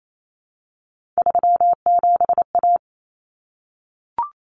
Morse code, heard as a keyed tone, about 28 wpm, 700 Hz. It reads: 37A
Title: Common Cut Numbers - 3-Digit 28wpm